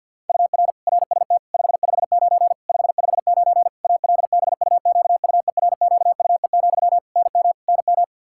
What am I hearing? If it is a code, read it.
Morse code, heard as a keyed tone, about 52 wpm, 700 Hz. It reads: UR RST 559 559 ALBUQUERQUE, NM NM
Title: Practice QSO - Exchange 1 - Beginning - 50wpm